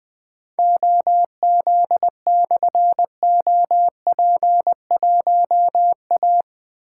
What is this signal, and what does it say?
Morse code, heard as a keyed tone, about 21 wpm, 700 Hz. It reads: OZ/OP1A
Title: Difficult Call Signs 20wpm